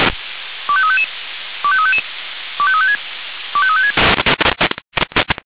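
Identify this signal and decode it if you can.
Five-tone selective call (ZVEI), heard as quick runs of five tones, tones of about 70 ms. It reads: 25355 25355 25356 25356
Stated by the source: ZVEI five-tone call 25343 received by SDR.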